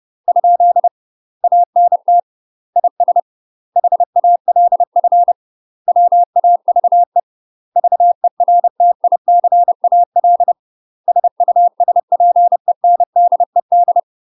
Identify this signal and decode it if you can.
Morse code, heard as a keyed tone, about 30 wpm, 700 Hz. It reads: ? ANT IS HALF WAVE VERTICAL SUSPENDED
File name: UK MASTER QSOs TRAINING / 30WPM / UK MASTER QSOs 0021